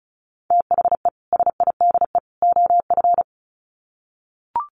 Morse code, heard as a keyed tone, about 34 wpm, 700 Hz. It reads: THE SIDE OF
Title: Sets of 3 Words 35wpm